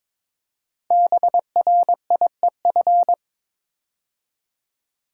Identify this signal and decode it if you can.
Morse code, heard as a keyed tone, about 23 wpm, 700 Hz. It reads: BRIEF
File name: 5 Letter Words - First 3 Letters Given - 22wpm